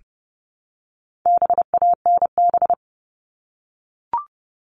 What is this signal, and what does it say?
Morse code, heard as a keyed tone, about 30 wpm, 700 Hz. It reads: BANB
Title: All Cut Numbers - 4-Digit 30wpm